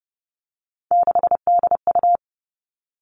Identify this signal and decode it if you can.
Morse code, heard as a keyed tone, about 30 wpm, 700 Hz. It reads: 6DU
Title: All Cut Numbers - 3-Digit 30wpm